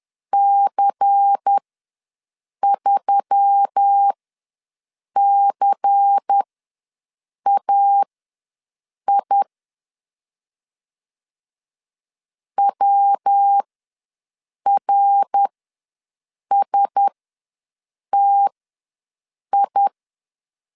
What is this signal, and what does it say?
Morse code, heard as a keyed tone, about 10 wpm, spaced as for 6 wpm, 800 Hz. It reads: C3CAI WRSTI
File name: LEZIONI / 5 LBVC / 5 LBVC ws 10wpm 3733